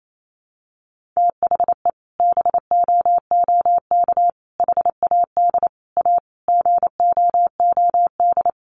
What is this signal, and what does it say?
Morse code, heard as a keyed tone, about 28 wpm, 700 Hz. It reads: THE BOOK HAD A GOOD
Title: Sentences from Top 1400 Words 28wpm